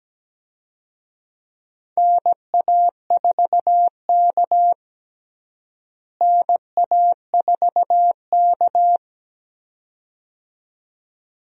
Morse code, heard as a keyed tone, about 17 wpm, 700 Hz. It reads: NA4K NA4K
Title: USA Call Signs - 4 Characters - Repeated 2x - 17wpm